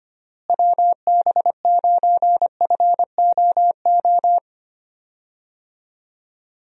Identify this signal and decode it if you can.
Morse code, heard as a keyed tone, about 25 wpm, 700 Hz. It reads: WB9FOO
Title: USA Call Signs - 6 Characters - Repeated 3x - 25wpm